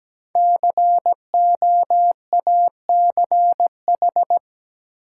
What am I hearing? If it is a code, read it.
Morse code, heard as a keyed tone, about 17 wpm, 700 Hz. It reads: COACH
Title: Five Letter Words - 17wpm